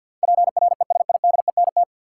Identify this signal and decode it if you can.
Morse code, heard as a keyed tone, about 53 wpm, 700 Hz. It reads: PRESIDENT
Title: Sentences from Top 1000 Words - New Word Drill - 50wpm